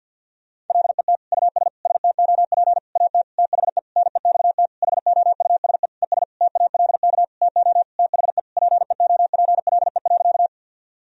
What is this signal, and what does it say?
Morse code, heard as a keyed tone, about 50 wpm, 700 Hz. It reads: LET US STOP AT THE NEXT HOUSE ES TALK TO THE PEOPLE.
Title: Sentences from Top 400 Words and Abbreviations 50wpm